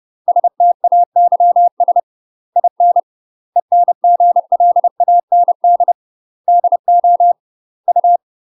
Morse code, heard as a keyed tone, about 30 wpm, 700 Hz. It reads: STAYS IN ENGLAND DO U
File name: UK MASTER QSOs TRAINING / 30WPM / UK MASTER QSOs 0040